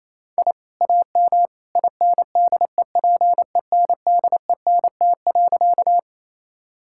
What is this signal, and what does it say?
Morse code, heard as a keyed tone, about 28 wpm, 700 Hz. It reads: I AM INDEPENDENT.
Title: Sentences from Top 1000 Words - Review Part 6 - 28wpm